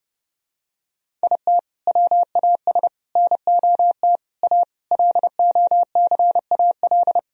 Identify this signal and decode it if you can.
Morse code, heard as a keyed tone, about 30 wpm, 700 Hz. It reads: IT WAS NOT A LOCAL
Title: Sentences from Top 1000 Words - Review Part 3 - 30wpm